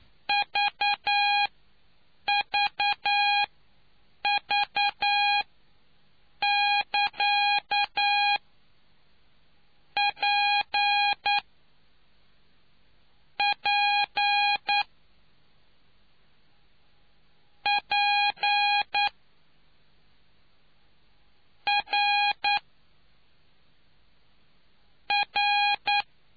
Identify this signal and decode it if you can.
Morse code, heard as a keyed tone, about 9 wpm, spaced as for 5 wpm, 820 Hz. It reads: VVV<KA>PP P R R